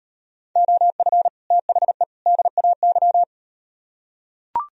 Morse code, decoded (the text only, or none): OF THE DAY